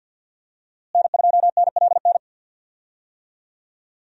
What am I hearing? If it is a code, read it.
Morse code, heard as a keyed tone, about 50 wpm, 700 Hz. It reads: N2NLN
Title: USA Call Signs - 5 Characters - Repeated 2x - 50wpm